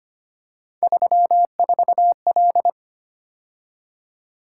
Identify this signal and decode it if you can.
Morse code, heard as a keyed tone, about 29 wpm, 700 Hz. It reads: 34L
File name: Speed-Racing - Three Letter-Number Combinations - 25wpm